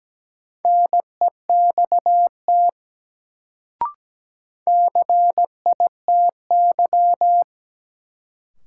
Morse code, heard as a keyed tone, about 17 wpm, 700 Hz. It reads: NEXT CITY
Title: Sentences from Top 300 Words - New Word Drill - 17wpm